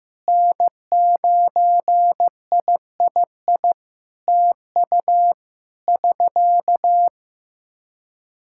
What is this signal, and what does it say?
Morse code, heard as a keyed tone, about 15 wpm, 700 Hz. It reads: N9III TU <SK>
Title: Practice QSO - Exchange 4 - Ending - 15wpm